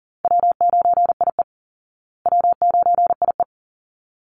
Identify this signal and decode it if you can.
Morse code, heard as a keyed tone, about 40 wpm, 700 Hz. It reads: W9IE W9IE
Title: Call Signs Repeated 2x - 40wpm